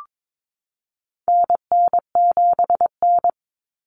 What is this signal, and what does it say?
Morse code, heard as a keyed tone, about 22 wpm, 700 Hz. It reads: NN7N
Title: Common Cut Numbers - 4-Digit 22wpm